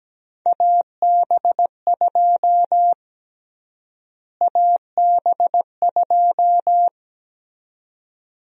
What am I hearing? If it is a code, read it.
Morse code, heard as a keyed tone, about 17 wpm, 700 Hz. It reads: AB2 AB2
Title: USA Call Sign Prefixes - Repeated 2x - 17wpm